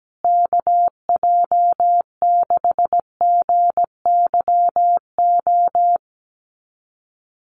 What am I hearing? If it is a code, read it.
Morse code, heard as a keyed tone, about 17 wpm, 700 Hz. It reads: KJ6GYO
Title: USA Call Signs Repeated 2x 17wpm